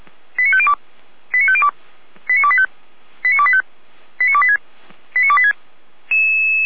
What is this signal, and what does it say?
Five-tone selective call (ZVEI), heard as quick runs of five tones, tones of about 70 ms. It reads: 89502 89502 89285 89285 89286 89286